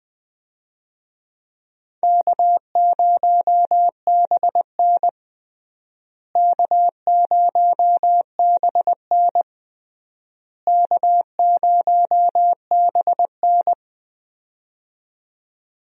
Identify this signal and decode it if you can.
Morse code, heard as a keyed tone, about 20 wpm, 700 Hz. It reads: K0BN K0BN K0BN
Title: USA Call Signs - 4 Characters - Repeated 3x - 20wpm